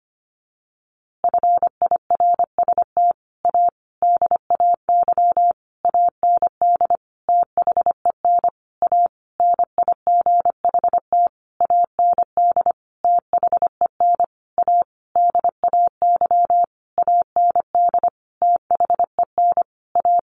Morse code, heard as a keyed tone, about 25 wpm, 700 Hz. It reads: FIRST A DAY AND THEN A NIGHT AND THEN A DAY AND THEN A
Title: A Sound Of Thunder 25wpm